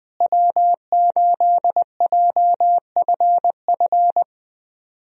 Morse code, heard as a keyed tone, about 21 wpm, 700 Hz. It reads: W8JFF
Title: USA Call Signs - 5 Characters - Repeated 3x - 20wpm